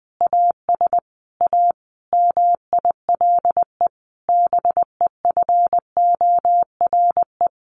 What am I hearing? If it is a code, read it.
Morse code, heard as a keyed tone, about 20 wpm, 700 Hz. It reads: AS A MILE BEFORE
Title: Sentences from Top 1100 Words 20wpm